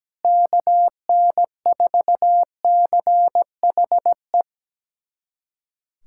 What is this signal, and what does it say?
Morse code, heard as a keyed tone, about 17 wpm, 700 Hz. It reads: KN4CHE
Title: USA Callsign DE Callsign - 6 Characters - 17wpm